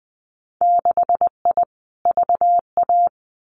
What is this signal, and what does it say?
Morse code, heard as a keyed tone, about 20 wpm, 700 Hz. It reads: 6I VA
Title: Winter Field Day Exchanges Encore 20wpm